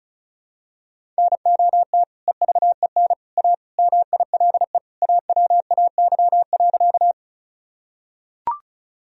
Morse code, heard as a keyed tone, about 36 wpm, 700 Hz. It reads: NOT EVEN A MILE AWAY.
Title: Sentences from Top 1000 Words - Review Part 1 - 35wpm